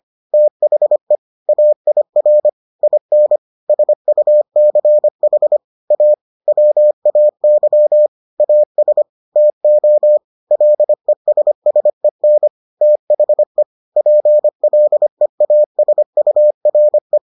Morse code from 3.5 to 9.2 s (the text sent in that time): SUCH A WAY AS